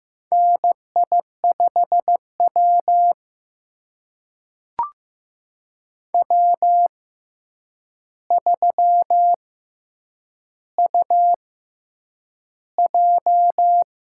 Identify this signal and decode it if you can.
Morse code, heard as a keyed tone, about 15 wpm, spaced as for 8 wpm, 700 Hz. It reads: NI5W W3UJ
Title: USA Call Signs - 4 Characters - 6x Character Spacing - 15wpm